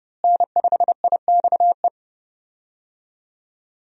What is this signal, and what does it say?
Morse code, heard as a keyed tone, about 30 wpm, 700 Hz. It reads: N5IXE
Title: USA Call Signs - 5 Characters - Repeated 3x - 30wpm